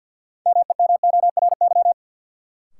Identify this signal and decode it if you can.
Morse code, heard as a keyed tone, about 46 wpm, 700 Hz. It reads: MEMORY
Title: Sentences from Top 1300 Words - New Word Drill Alt - 50wpm